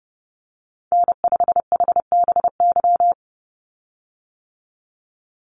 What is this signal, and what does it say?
Morse code, heard as a keyed tone, about 30 wpm, 700 Hz. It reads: N5HBY
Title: USA Call Signs 30wpm